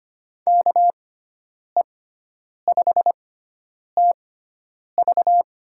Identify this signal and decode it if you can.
Morse code, heard as a keyed tone, about 25 wpm, spaced as for 9 wpm, 700 Hz. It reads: KE5TV
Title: USA Call Signs - 6 Characters - 6x Character Spacing - 25wpm